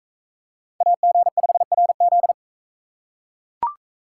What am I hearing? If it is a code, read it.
Morse code, heard as a keyed tone, about 42 wpm, 700 Hz. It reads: AG5RZ